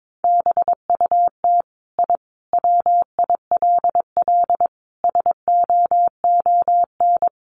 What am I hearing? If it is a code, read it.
Morse code, heard as a keyed tone, about 22 wpm, 700 Hz. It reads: BUT I WILL SOON
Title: Sentences from Top 1300 Words 22wpm